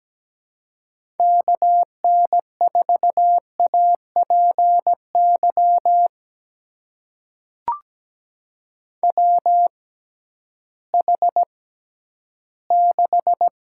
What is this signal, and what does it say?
Morse code, heard as a keyed tone, about 17 wpm, spaced as for 11 wpm, 700 Hz. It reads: KN4APY WH6
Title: USA Call Signs - 6 Characters - 6x Character Spacing - 17wpm